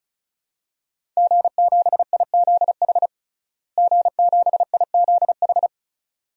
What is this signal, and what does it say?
Morse code, heard as a keyed tone, about 36 wpm, 700 Hz. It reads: G7IZH G7IZH
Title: DXCC-100 Call Signs - 5 Characters - Repeated 2x - 35wpm